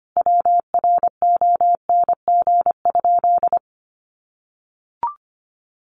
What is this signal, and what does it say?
Morse code, heard as a keyed tone, about 25 wpm, 700 Hz. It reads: WRONG?
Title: Sentences from Top 900 Words 25wpm